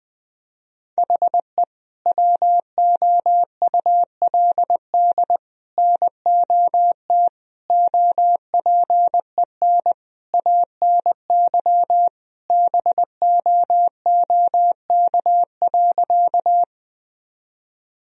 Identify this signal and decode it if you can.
Morse code, heard as a keyed tone, about 20 wpm, 700 Hz. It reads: HE WOULD NOT OPEN ANY BOOK.
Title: Sentences from Top 1000 Words - Review Part 1 - 20wpm